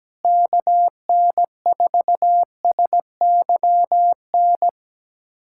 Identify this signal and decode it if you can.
Morse code, heard as a keyed tone, about 17 wpm, 700 Hz. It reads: KN4SYN
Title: USA Callsign DE Callsign - 6 Characters - 17wpm